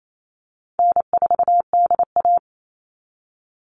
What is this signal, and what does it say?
Morse code, heard as a keyed tone, about 28 wpm, 700 Hz. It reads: N4DA
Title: NA SOTA Chaser Callsigns 28wpm